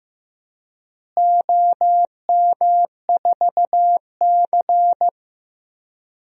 Morse code, heard as a keyed tone, about 15 wpm, 700 Hz. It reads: OM4C
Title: DXCC-100 Call Signs - 4 Characters - Repeated 3x - 15wpm